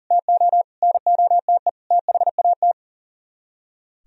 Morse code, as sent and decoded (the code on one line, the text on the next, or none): - --- / -. --- - . / - .... .- -
TO NOTE THAT